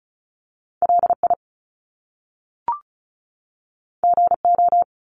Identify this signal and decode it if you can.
Morse code, heard as a keyed tone, about 36 wpm, 700 Hz. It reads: LI GO
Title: Top Two Letter Combinations in Words 35wpm